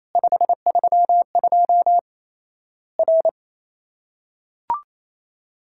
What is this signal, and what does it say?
Morse code, heard as a keyed tone, about 28 wpm, 700 Hz. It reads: 532 R
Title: Worked All Europe DX Contest - QTC Exchange - 28wpm